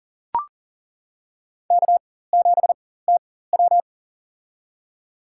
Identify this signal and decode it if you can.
Morse code, heard as a keyed tone, about 40 wpm, spaced as for 19 wpm, 700 Hz. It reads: K7TW
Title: USA Call Signs - 4 Characters - 4x Character Spacing - 40wpm